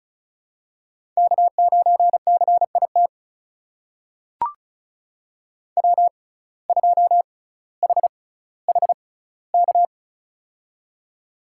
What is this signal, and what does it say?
Morse code, heard as a keyed tone, about 34 wpm, spaced as for 18 wpm, 700 Hz. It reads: K9CIT W2HHK
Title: USA Call Signs - 6x Character Spacing - 35wpm